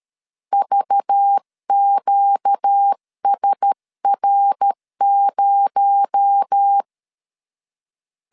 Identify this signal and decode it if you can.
Morse code, heard as a keyed tone, about 13 wpm, 800 Hz. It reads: VQSR0
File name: CW / DET / 1 6 1500C S C 12wpm 3533